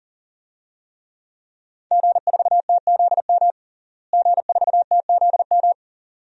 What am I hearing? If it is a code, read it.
Morse code, heard as a keyed tone, about 40 wpm, 700 Hz. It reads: G4TZM G4TZM
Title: DXCC-100 Call Signs - 5 Characters - Repeated 2x - 40wpm